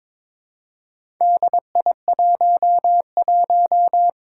Morse code, heard as a keyed tone, about 22 wpm, 700 Hz. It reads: DI11
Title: Four Challenging Letter-Number Combinations 22wpm